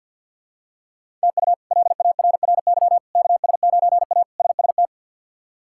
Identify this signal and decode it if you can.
Morse code, heard as a keyed tone, about 52 wpm, 700 Hz. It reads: TU LARRY KS9A SST
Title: K1USN SST QSO Practice - 50wpm